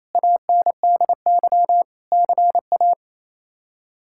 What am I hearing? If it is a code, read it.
Morse code, heard as a keyed tone, about 28 wpm, 700 Hz. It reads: ANDY CA
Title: K1USN SST QSO Practice - 28wpm